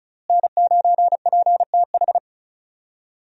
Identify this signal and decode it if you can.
Morse code, heard as a keyed tone, about 36 wpm, 700 Hz. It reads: N9PTH
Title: USA Call Signs - 5 Characters - Repeated 2x - 35wpm